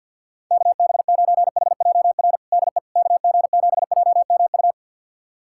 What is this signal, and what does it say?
Morse code, heard as a keyed tone, about 52 wpm, 700 Hz. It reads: KB9HJR DE KG7JMU
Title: USA Callsign DE Callsign - 6 Characters - 50wpm